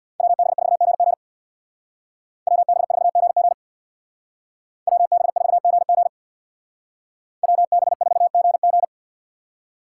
Morse code, heard as a keyed tone, about 58 wpm, 700 Hz. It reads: WB4GG WB4GG WB4GG WB4GG
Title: Speed Racing - USA 2x2 Call Signs 50wpm